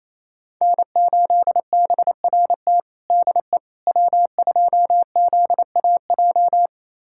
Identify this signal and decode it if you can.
Morse code, heard as a keyed tone, about 27 wpm, 700 Hz. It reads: N8BRT DE W2ZAJ
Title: USA Callsign DE Callsign - 5 Characters - 28wpm